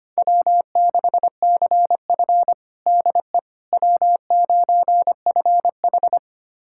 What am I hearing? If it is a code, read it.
Morse code, heard as a keyed tone, about 25 wpm, 700 Hz. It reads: W6CF DE W9FH
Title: USA Callsign DE Callsign - 4 Characters - 25wpm